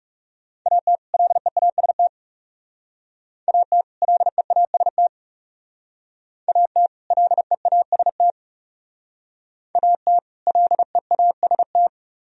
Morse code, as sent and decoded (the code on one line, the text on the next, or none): .- - / .-.. .- ... - / .- - / .-.. .- ... - / .- - / .-.. .- ... - / .- - / .-.. .- ... -
AT LEAST AT LEAST AT LEAST AT LEAST